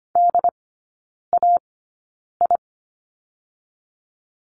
Morse code, heard as a keyed tone, about 25 wpm, spaced as for 9 wpm, 690 Hz. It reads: DAI
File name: Lesson 14e - Number 2 - Three Characters 25wpm